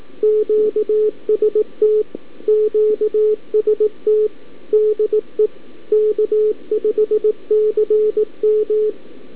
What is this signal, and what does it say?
Morse code, heard as a keyed tone, about 18 wpm, 430 Hz, with radio noise. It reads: QST QST DE K5CM
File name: k5cm-20mNE Callup-by-abi